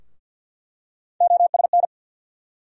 Morse code, heard as a keyed tone, about 47 wpm, 700 Hz. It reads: OSN
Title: USA Call Sign Postfixes - 3 Letters 50wpm